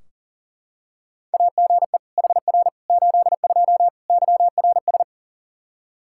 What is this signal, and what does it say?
Morse code, heard as a keyed tone, about 40 wpm, 700 Hz. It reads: AGE HR 82 YRS
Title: K1USN SST QSO Practice 40wpm